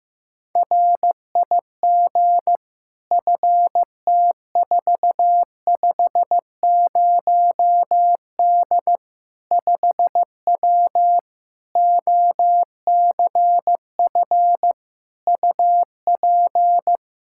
Morse code, heard as a keyed tone, about 15 wpm, 700 Hz. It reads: RIG FT450D 5W OCF UP
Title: K1USN SST QSO Practice 15wpm